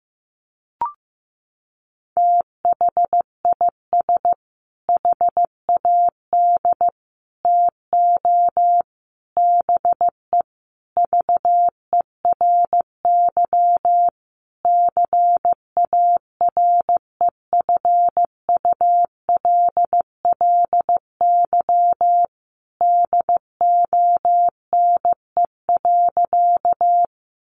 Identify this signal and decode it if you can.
Morse code, heard as a keyed tone, about 15 wpm, 700 Hz. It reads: THIS HAD TO BE VERY CAREFULLY DONE.
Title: Sentences from Top 1300 Words 15wpm